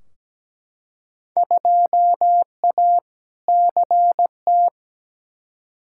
Morse code, heard as a keyed tone, about 17 wpm, 700 Hz. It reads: 2A CT
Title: Field Day Exchanges - 17wpm